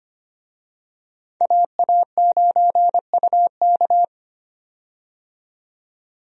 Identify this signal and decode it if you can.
Morse code, heard as a keyed tone, about 25 wpm, 700 Hz. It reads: AA9UK
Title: USA Call Signs - 5 Characters - 25wpm